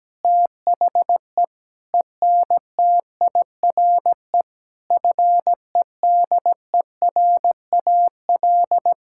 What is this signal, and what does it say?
Morse code, heard as a keyed tone, about 17 wpm, 700 Hz. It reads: THE ENTIRE FEDERAL
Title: Sentences from Top 1000 Words - Review Part 5 - 17wpm